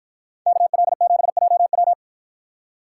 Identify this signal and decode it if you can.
Morse code, heard as a keyed tone, about 53 wpm, 700 Hz. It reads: KL7JW